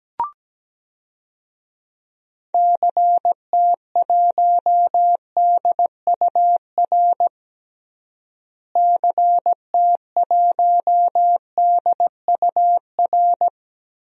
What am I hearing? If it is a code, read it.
Morse code, heard as a keyed tone, about 17 wpm, 700 Hz. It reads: CT1DUR CT1DUR
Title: DXCC-100 Call Signs - 6 Characters - Repeated 3x - 17wpm